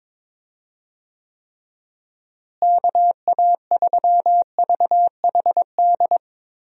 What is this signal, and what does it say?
Morse code, heard as a keyed tone, about 22 wpm, 700 Hz. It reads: KA3VHD
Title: USA Call Signs - 6 Characters - Repeated 2x - 22wpm